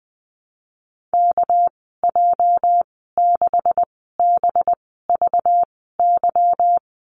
Morse code, heard as a keyed tone, about 19 wpm, spaced as for 14 wpm, 700 Hz. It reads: KJ6BVY
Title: 219 Lesson 35g - Number 0 - Call Signs 20wpm